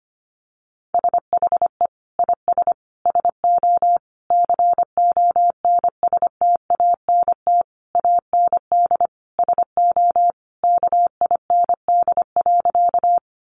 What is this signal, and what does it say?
Morse code, heard as a keyed tone, about 25 wpm, 700 Hz. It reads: SHE IS SO CONSTANT AND SO KIND.
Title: Sentences from Top 1500 Words 25wpm